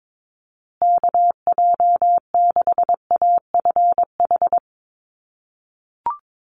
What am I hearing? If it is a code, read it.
Morse code, heard as a keyed tone, about 22 wpm, 690 Hz. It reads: KJ6AFH
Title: USA Call Signs Repeated 3x 22wpm